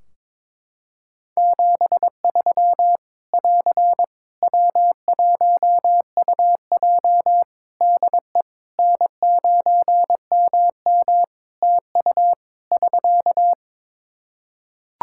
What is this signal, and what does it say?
Morse code, heard as a keyed tone, about 22 wpm, 700 Hz. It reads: 73 <AR> W1UJ DE N9MM TU <SK>
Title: Practice QSO - Exchange 4 - Ending - 22wpm